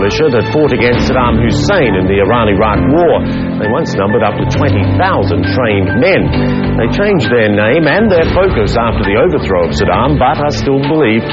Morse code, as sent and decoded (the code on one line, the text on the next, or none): none